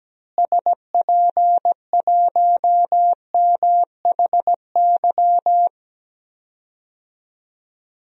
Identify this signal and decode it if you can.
Morse code, heard as a keyed tone, about 17 wpm, 700 Hz. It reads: SP1MHY
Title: DXCC-100 Call Signs - 6 Characters - Repeated 3x - 17wpm